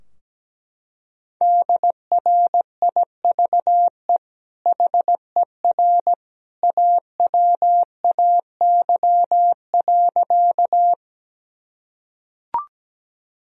Morse code, heard as a keyed tone, about 17 wpm, 700 Hz. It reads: DRIVE HER AWAY.
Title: Sentences from Top 1000 Words - Review Part 2 - 17wpm